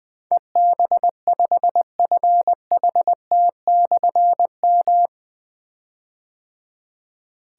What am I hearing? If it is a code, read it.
Morse code, heard as a keyed tone, about 20 wpm, 700 Hz. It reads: EB5FHT/M
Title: Difficult Call Signs 20wpm